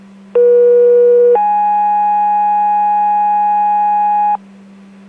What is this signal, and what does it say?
Two-tone page, heard as a one-second tone, then a three-second tone, about 480 then 830 Hz, with noise. It is heard from 0.3 to 4.4 s.